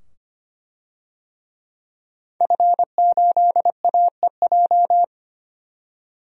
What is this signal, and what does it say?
Morse code, heard as a keyed tone, about 25 wpm, 700 Hz. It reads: F8AEJ
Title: DXCC-100 Call Signs - 5 Characters - 25wpm